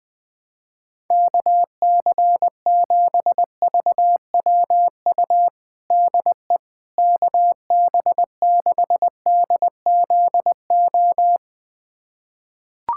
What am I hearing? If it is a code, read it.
Morse code, heard as a keyed tone, about 20 wpm, 700 Hz. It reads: KC7VWU DE KB6DZO
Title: USA Callsign DE Callsign 20wpm